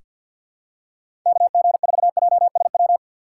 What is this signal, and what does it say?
Morse code, heard as a keyed tone, about 46 wpm, 700 Hz. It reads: KG4JSW
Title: USA Call Signs - 6 Characters - 6x Character Spacing - 50wpm